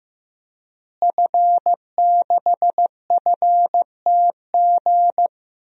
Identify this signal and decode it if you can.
Morse code, heard as a keyed tone, about 15 wpm, 700 Hz. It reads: F6FTG
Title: DXCC-100 Call Signs - 5 Characters - Repeated 2x - 15wpm